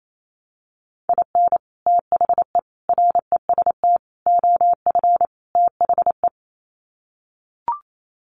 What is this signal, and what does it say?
Morse code, heard as a keyed tone, about 28 wpm, 700 Hz. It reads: IN THE REST OF THE
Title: Sets of 5 Words 28wpm